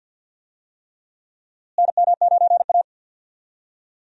Morse code, heard as a keyed tone, about 53 wpm, 700 Hz. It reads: NM9A
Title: USA Call Signs - 4 Characters - Repeated 3x - 50wpm